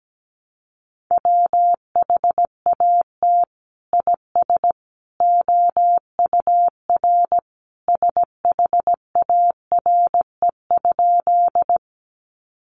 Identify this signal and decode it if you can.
Morse code, heard as a keyed tone, about 17 wpm, 700 Hz. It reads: WHAT IS OUR SHARE?
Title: Sentences from Top 900 Words 17wpm